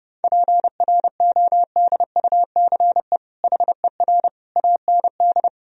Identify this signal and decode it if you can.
Morse code, heard as a keyed tone, about 30 wpm, 700 Hz. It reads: PRODUCE HER AND
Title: Sentences from Top 1000 Words - Review Part 3 - 30wpm